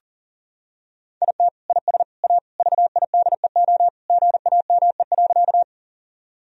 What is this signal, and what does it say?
Morse code, heard as a keyed tone, about 40 wpm, 690 Hz. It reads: IT IS A VIDEO GAME.
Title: Sentences from Top 1000 Words - Review Part 4 - 40wpm